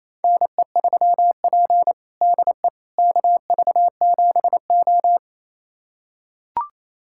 Morse code, heard as a keyed tone, about 28 wpm, 700 Hz. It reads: NE3P DE KV7O
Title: USA Callsign DE Callsign - 4 Characters - 28wpm